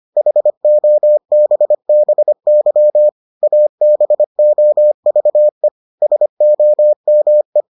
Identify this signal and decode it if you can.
Morse code, heard as a keyed tone, about 25 wpm, 600 Hz. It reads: HOBBY ABOVE SOME